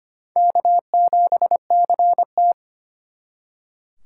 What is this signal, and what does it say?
Morse code, heard as a keyed tone, about 33 wpm, 700 Hz. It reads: K7CT